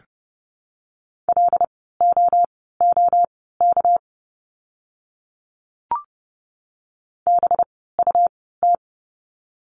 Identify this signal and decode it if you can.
Morse code, heard as a keyed tone, about 30 wpm, spaced as for 17 wpm, 700 Hz. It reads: LOOK BUT
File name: course / 30:10wpm / 237 Lesson 38f - Letter q - Words 30wpm